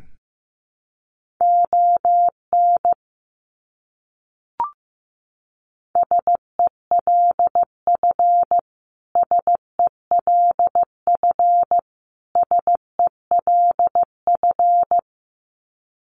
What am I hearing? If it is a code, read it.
Morse code, heard as a keyed tone, about 15 wpm, 700 Hz. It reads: ON SELF SELF SELF
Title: Top 300 Words Repeated 15wpm